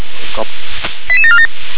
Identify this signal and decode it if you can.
Five-tone selective call (ZVEI), heard as quick runs of five tones, tones of about 70 ms, with speech. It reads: 88747